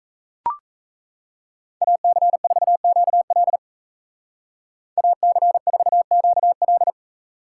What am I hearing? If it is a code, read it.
Morse code, heard as a keyed tone, about 34 wpm, 700 Hz. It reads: AC4QL AC4QL
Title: Speed Racing - USA 2x2 Call Signs 28wpm